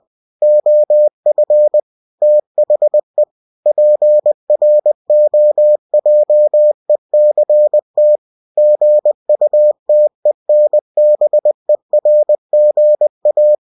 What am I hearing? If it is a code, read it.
Morse code, heard as a keyed tone, about 20 wpm, 600 Hz. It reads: OF THE PROJECT GUTENBERGA
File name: cw-books / winnie-the-pooh / winnie-20wpm-page-0116